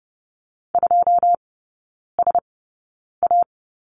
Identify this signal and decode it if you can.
Morse code, heard as a keyed tone, about 30 wpm, spaced as for 9 wpm, 700 Hz. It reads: 2SA
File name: course / 30:10wpm / 087 Lesson 15e - Number 5 - Three Characters 30wpm